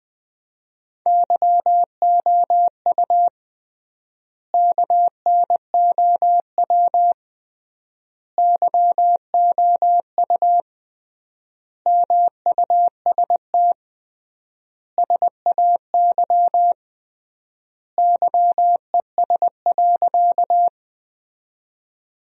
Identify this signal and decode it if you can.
Morse code, heard as a keyed tone, about 20 wpm, 700 Hz. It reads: YOU KNOW YOU MUST SAY YES.
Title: 3x Word-Spacing - Sentences from Top 500 Words - 20wpm